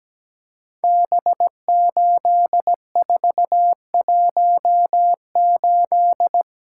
Morse code, heard as a keyed tone, about 17 wpm, 700 Hz. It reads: B8418
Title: Code Groups - 5 Letter:Numbers - 17wpm